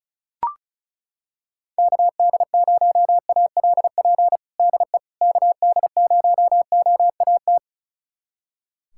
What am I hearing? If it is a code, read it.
Morse code, heard as a keyed tone, about 36 wpm, 700 Hz. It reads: KD0ALP DE KD0OAT
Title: USA Callsign DE Callsign 35wpm